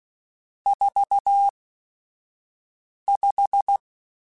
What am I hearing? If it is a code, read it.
Morse code, heard as a keyed tone, about 16 wpm, spaced as for 5 wpm, 800 Hz. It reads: 45
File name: Weather Sets / Weather Sets-mp3s / Weather 3 17 @5